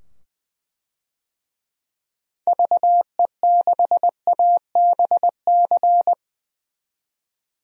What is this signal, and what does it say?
Morse code, heard as a keyed tone, about 20 wpm, 700 Hz. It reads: VE6ABC
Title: DXCC-100 Call Signs - 6 Characters - 20wpm